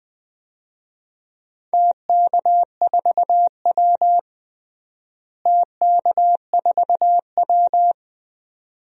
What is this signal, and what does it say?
Morse code, heard as a keyed tone, about 21 wpm, 700 Hz. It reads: TK4W TK4W
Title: DXCC-100 Call Signs - 4 Characters - Repeated 2x - 20wpm